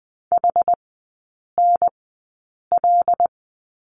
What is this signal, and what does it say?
Morse code, heard as a keyed tone, about 20 wpm, spaced as for 8 wpm, 700 Hz. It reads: HNL
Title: 100 Lesson 17e - Letter u - Three Characters 20wpm